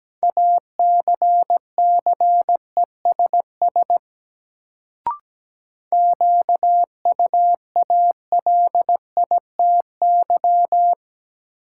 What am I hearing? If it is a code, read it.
Morse code, heard as a keyed tone, about 17 wpm, 700 Hz. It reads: ACCESS QUALITY
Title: Sentences from Top 1000 Words - Review Part 4 - New Word Drill - 17wpm